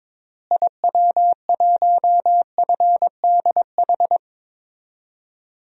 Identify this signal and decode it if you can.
Morse code, heard as a keyed tone, about 22 wpm, 700 Hz. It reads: IW1FDH
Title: DXCC-100 Call Signs - 6 Characters - Repeated 3x - 22wpm